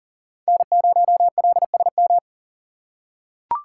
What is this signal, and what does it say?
Morse code, heard as a keyed tone, about 38 wpm, 690 Hz. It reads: N0LSM
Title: USA Call Signs - 3x Character Spacing - 40wpm